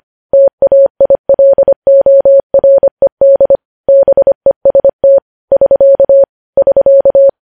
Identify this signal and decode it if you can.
Morse code, heard as a keyed tone, about 25 wpm, 550 Hz. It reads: TAILORED BEST <SK> <SK>
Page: news-headlines-in-morse-code-at-25-wpm